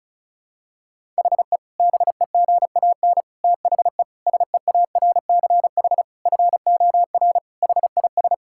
Morse code, heard as a keyed tone, about 36 wpm, 700 Hz. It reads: HE BEGAN THE SEARCH FOR HIS
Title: Sentences from Top 1000 Words - Review Part 5 - 35wpm